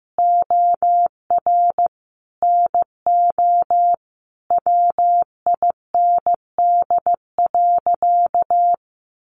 Morse code, heard as a keyed tone, about 15 wpm, 700 Hz. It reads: OR NO WIND.
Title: Sentences from Top 1400 Words 15wpm